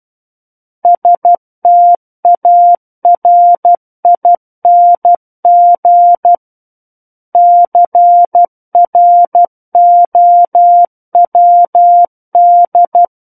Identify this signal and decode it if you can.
Morse code, heard as a keyed tone, about 12 wpm, 700 Hz. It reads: STARING CROWD
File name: Book One / 03-Chapter 3 - 12 WPM